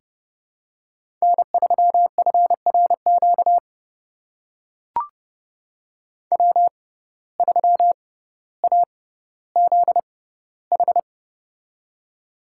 Morse code, heard as a keyed tone, about 30 wpm, spaced as for 15 wpm, 700 Hz. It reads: N3FRQ W3AZH
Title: USA Call Signs - 5 Characters - 6x Character Spacing - 30wpm